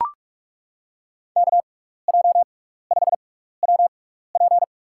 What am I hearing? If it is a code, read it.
Morse code, heard as a keyed tone, about 42 wpm, spaced as for 15 wpm, 700 Hz. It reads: KJ5WP
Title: USA Call Signs - 5 Characters - 6x Character Spacing - 45wpm